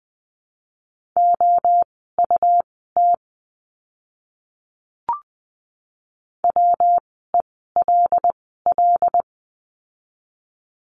Morse code, heard as a keyed tone, about 20 wpm, spaced as for 14 wpm, 700 Hz. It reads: OUT WELL
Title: 206 Lesson 33f - Letter j - Words 20wpm